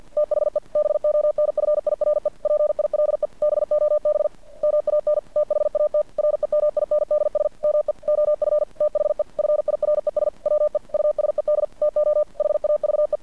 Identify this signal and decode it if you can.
Morse code, heard as a keyed tone, about 50 wpm, 600 Hz. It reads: THE BONFIRE PILE BOB MAN THAT REMINDS ME OF THE FIRES WE USED TO HAVE
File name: archived / CRSnet